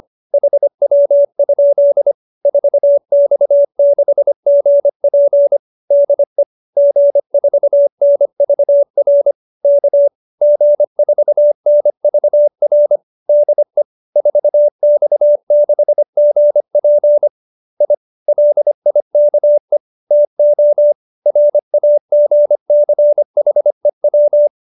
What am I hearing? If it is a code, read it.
Morse code, heard as a keyed tone, about 25 wpm, 590 Hz. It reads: HW? 4X6GP DE G4NVR K G4NVR DE 4X6GP I LIKE TO RAGCHEW
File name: QSO315 25